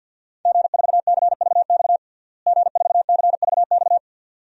Morse code, heard as a keyed tone, about 52 wpm, 700 Hz. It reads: G4CVX G4CVX
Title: DXCC-100 Call Signs - 5 Characters - Repeated 2x - 50wpm